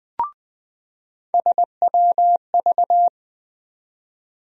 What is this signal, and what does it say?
Morse code, heard as a keyed tone, about 20 wpm, 700 Hz. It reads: SWV
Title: USA Call Sign Postfixes - 3 Letters 20wpm